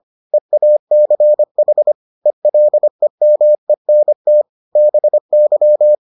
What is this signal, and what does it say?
Morse code, heard as a keyed tone, about 25 wpm, 600 Hz. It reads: EACH ELEMENT BY